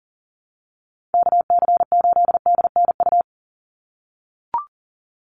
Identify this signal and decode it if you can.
Morse code, heard as a keyed tone, about 40 wpm, 700 Hz. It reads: KC8DNU
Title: USA Call Signs 40wpm